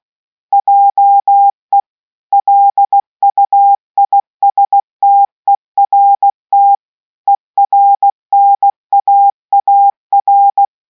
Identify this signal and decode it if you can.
Morse code, heard as a keyed tone, about 16 wpm, 800 Hz. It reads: JE LUISTERT ERNAAR